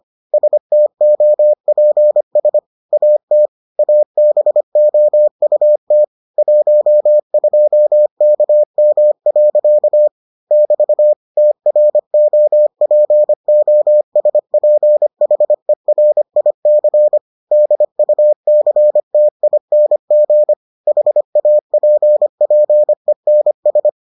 Morse code, heard as a keyed tone, about 25 wpm, 600 Hz. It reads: STOPS AT ABOUT 12KM. <BT> TROPOSPHERIC DUCTING HAPPENS